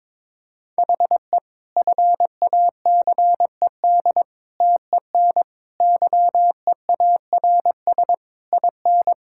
Morse code, heard as a keyed tone, about 22 wpm, 700 Hz. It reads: HE FACED TEN YEARS IN
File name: Sentences with 9-Word Phrases - 22wpm